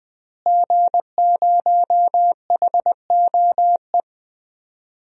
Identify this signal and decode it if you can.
Morse code, heard as a keyed tone, about 20 wpm, 700 Hz. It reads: G0HOE
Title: DXCC-100 Call Signs - 5 Characters - Repeated 3x - 20wpm